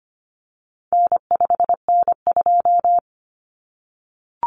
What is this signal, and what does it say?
Morse code, heard as a keyed tone, about 25 wpm, 700 Hz. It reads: N5N2
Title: Common Cut Numbers - 4-Digit 25wpm